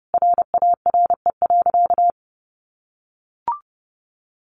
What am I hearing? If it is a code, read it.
Morse code, heard as a keyed tone, about 30 wpm, 700 Hz. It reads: RARE.
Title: Sentences from Top 1100 Words 30wpm